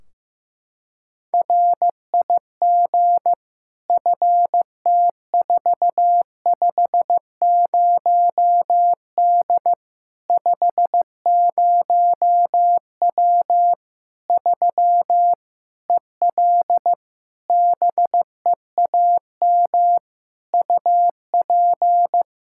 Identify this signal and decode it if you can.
Morse code, heard as a keyed tone, about 16 wpm, 700 Hz. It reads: RIG FT450D 50W 3 EL BEAM UP
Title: K1USN SST QSO Practice 15wpm